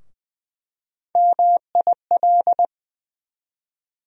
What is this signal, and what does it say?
Morse code, heard as a keyed tone, about 20 wpm, 700 Hz. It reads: MIL
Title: USA Call Sign Postfixes - 3 Letters 20wpm